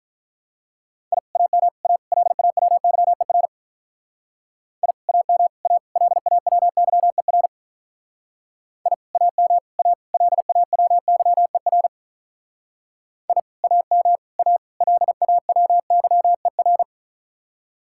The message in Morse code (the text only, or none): I AM A LAWYER I AM A LAWYER I AM A LAWYER I AM A LAWYER